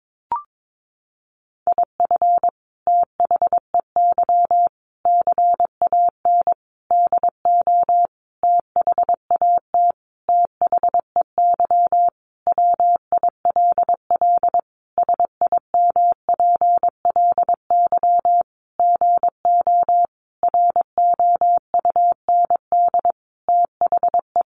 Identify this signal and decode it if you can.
Morse code, heard as a keyed tone, about 22 wpm, 700 Hz. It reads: IF THEY CAN DO THAT THEY WILL SIMPLY GO ROUND THE
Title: Sentences from Top 600 Words 22wpm